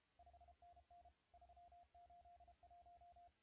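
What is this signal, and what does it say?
Morse code, heard as a keyed tone, about 33 wpm, 700 Hz, with radio noise. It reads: 5NN 381
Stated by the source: CW contest audio recorded at LA4C during the SAC CW contest.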